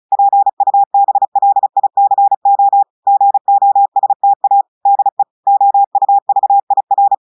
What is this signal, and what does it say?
Morse code, heard as a keyed tone, about 36 wpm, 800 Hz. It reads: PUBLICO GOSTA DE OUVIR
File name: portugais texte en clair 2 vitesse 35 mots minutes